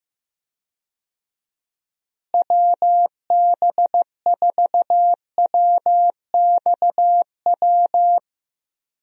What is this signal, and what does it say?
Morse code, heard as a keyed tone, about 15 wpm, 700 Hz. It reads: WB4WXW
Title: USA Call Signs - 6 Characters - Repeated 2x - 15wpm